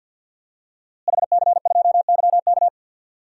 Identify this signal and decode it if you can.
Morse code, heard as a keyed tone, about 53 wpm, 700 Hz. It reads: HC2YK
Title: DXCC-100 Call Signs - 5 Characters - Repeated 2x - 50wpm